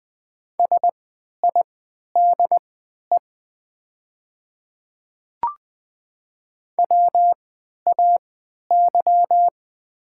Morse code, heard as a keyed tone, about 20 wpm, spaced as for 11 wpm, 700 Hz. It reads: SIDE WAY
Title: 3x Character Spacing - Top 100 Words - 20wpm